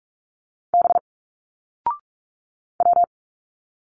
Morse code, heard as a keyed tone, about 44 wpm, 700 Hz. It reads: B W
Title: Single Letters 45wpm